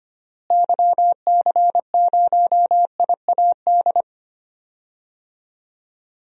Morse code, heard as a keyed tone, about 25 wpm, 700 Hz. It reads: YC0IAD
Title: DXCC-100 Call Signs - 6 Characters - Repeated 2x - 25wpm